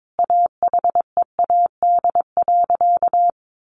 Morse code, heard as a keyed tone, about 22 wpm, 700 Hz. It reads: AHEAD.
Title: Sentences from Top 1100 Words 22wpm